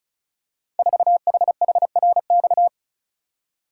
Morse code, heard as a keyed tone, about 35 wpm, 700 Hz. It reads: 4HHRX
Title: Code Groups - 5 Letter:Numbers - 35wpm